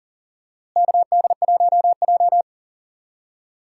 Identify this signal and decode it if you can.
Morse code, heard as a keyed tone, about 40 wpm, 700 Hz. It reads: KD1J
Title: USA Call Signs - 4 Characters - Repeated 2x - 40wpm